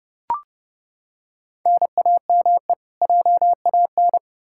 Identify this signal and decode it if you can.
Morse code, heard as a keyed tone, about 31 wpm, 700 Hz. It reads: NAME JAN
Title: Practice QSO - Exchange 1 - Beginning - 30wpm